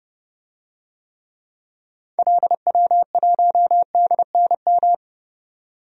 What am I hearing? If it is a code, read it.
Morse code, heard as a keyed tone, about 30 wpm, 700 Hz. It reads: LW1DNM
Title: DXCC-100 Call Signs - 6 Characters - 30wpm